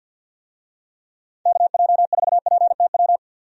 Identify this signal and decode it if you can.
Morse code, heard as a keyed tone, about 51 wpm, 700 Hz. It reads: KJ4PTW
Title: USA Call Signs - 6 Characters - 50wpm